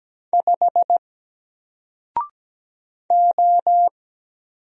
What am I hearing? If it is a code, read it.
Morse code, heard as a keyed tone, about 17 wpm, 700 Hz. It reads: 5 O
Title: Single Letter-Number Alt 17wpm